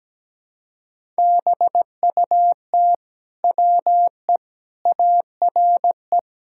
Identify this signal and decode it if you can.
Morse code, heard as a keyed tone, about 17 wpm, 700 Hz. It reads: BUT WE ARE
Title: Sentences from Top 1000 Words - Review Part 5 - 17wpm